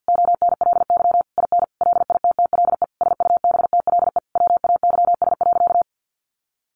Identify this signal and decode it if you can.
Morse code, heard as a keyed tone, about 50 wpm, 700 Hz. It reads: ONLY IN LITTLE SUBTLE WAYS.
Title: A Sound Of Thunder 50wpm